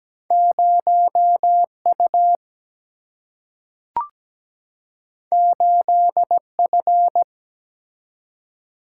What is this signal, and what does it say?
Morse code, heard as a keyed tone, about 17 wpm, 700 Hz. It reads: 0 U 8 F
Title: Single-Letter Single-Number 17wpm